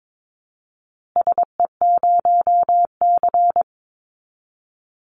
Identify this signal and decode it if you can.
Morse code, heard as a keyed tone, about 22 wpm, 700 Hz. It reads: SE0C
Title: Call Signs Repeated 2x - 22wpm